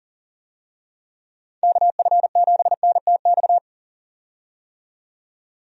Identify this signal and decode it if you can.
Morse code, heard as a keyed tone, about 40 wpm, 700 Hz. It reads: KF7NTX
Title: USA Call Signs - 6 Characters - 40wpm